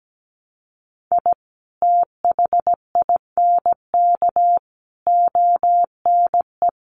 Morse code, heard as a keyed tone, about 17 wpm, 700 Hz. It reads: I THINK ONE
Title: Sets of 5 Words 17wpm